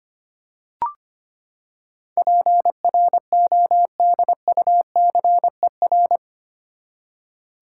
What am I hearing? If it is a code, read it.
Morse code, heard as a keyed tone, about 25 wpm, 700 Hz. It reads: PRODUCER
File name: Eight Letter Words - 25wpm